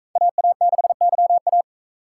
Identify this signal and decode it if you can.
Morse code, heard as a keyed tone, about 42 wpm, 700 Hz. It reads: AA6YA